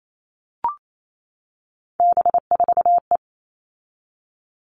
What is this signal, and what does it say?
Morse code, heard as a keyed tone, about 28 wpm, 700 Hz. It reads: B4E
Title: All Cut Numbers - 3-Digit 28wpm